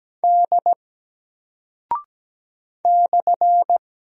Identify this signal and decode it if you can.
Morse code, heard as a keyed tone, about 17 wpm, 700 Hz. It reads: D /
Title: Single Character (Letter:Number:Punctuation:Prosign) 17wpm